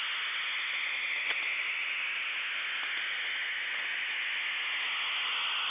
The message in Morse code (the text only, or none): none